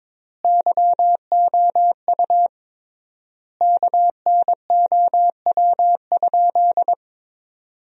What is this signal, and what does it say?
Morse code, heard as a keyed tone, about 22 wpm, 700 Hz. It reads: YOU KNOW?
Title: 3x Word-Spacing - Sentences from Top 500 Words - 22wpm